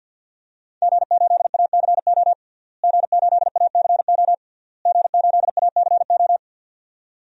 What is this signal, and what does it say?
Morse code, heard as a keyed tone, about 52 wpm, 700 Hz. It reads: G8ACO G8ACO G8ACO
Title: DXCC-100 Call Signs - 5 Characters - Repeated 3x - 50wpm